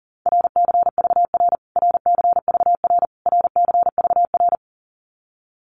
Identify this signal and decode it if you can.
Morse code, heard as a keyed tone, about 40 wpm, 700 Hz. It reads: RCVR RCVR RCVR
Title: Abbreviations Repeated 3x - 40wpm